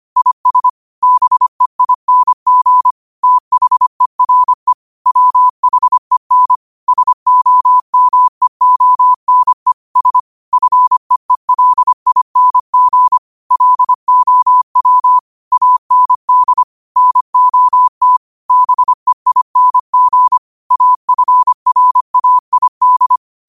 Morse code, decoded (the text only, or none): IS BEING THERE WHEN SOMEONE S FEELING LOW AND NOT BEING AFRAID